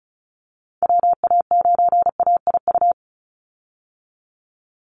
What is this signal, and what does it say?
Morse code, heard as a keyed tone, about 34 wpm, 700 Hz. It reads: WA9AIU
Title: USA Call Signs 35wpm